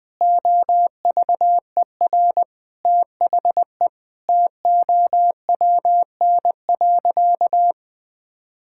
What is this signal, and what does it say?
Morse code, heard as a keyed tone, about 21 wpm, 700 Hz. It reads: OVER THE TOWN.
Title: Sentences with 3-Word Phrases - Spoken Beginning - 20wpm